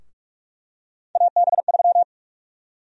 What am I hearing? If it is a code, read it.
Morse code, heard as a keyed tone, about 44 wpm, 700 Hz. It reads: AB3
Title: USA Call Sign Prefixes - Repeated 2x - 45wpm